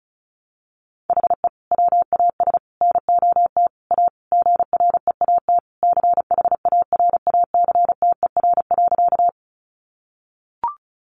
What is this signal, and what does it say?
Morse code, heard as a keyed tone, about 34 wpm, 700 Hz. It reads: HE WAS NOT A GREAT CHARACTER.
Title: Sentences from Top 1200 Words 35wpm